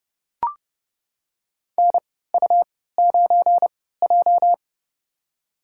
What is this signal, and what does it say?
Morse code, heard as a keyed tone, about 32 wpm, spaced as for 17 wpm, 700 Hz. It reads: NU9J
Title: USA Call Signs - 4 Characters - 3x Character Spacing - 30wpm